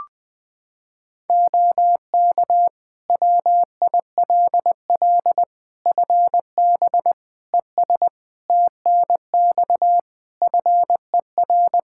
Morse code, heard as a keyed tone, about 21 wpm, 700 Hz. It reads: OK WILL FB ES TNX FER
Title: Practice QSO - Exchange 2 - Info - 20wpm